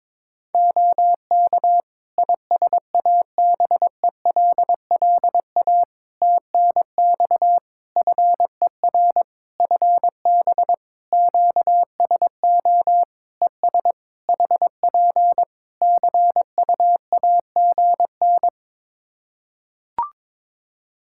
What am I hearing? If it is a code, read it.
Morse code, heard as a keyed tone, about 22 wpm, 700 Hz. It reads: OK ISABELLA TNX FER FB QSO ES HP CUAGN
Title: Practice QSO - Exchange 4 - Ending - 22wpm